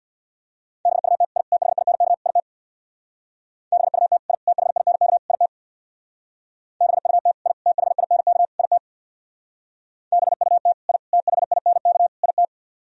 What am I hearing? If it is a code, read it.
Morse code, heard as a keyed tone, about 64 wpm, 700 Hz. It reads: BUT I THINK IT BUT I THINK IT BUT I THINK IT BUT I THINK IT